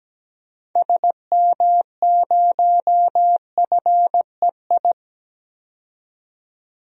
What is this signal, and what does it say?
Morse code, heard as a keyed tone, about 17 wpm, 700 Hz. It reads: SM0FEI
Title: DXCC-100 Call Signs - 6 Characters - 17wpm